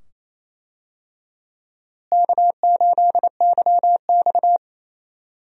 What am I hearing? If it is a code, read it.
Morse code, heard as a keyed tone, about 28 wpm, 700 Hz. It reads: K8YX
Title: USA Call Signs - 4 Characters - Repeated 2x - 28wpm